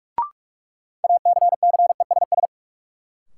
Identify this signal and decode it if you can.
Morse code, heard as a keyed tone, about 41 wpm, 700 Hz. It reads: ACCESS
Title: Sentences from Top 1000 Words - Review Part 4 - New Word Drill - 45wpm